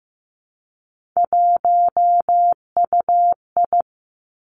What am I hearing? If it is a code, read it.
Morse code, heard as a keyed tone, about 15 wpm, 700 Hz. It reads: 1UI
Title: Three Challenging Letter-Number Combinations 15wpm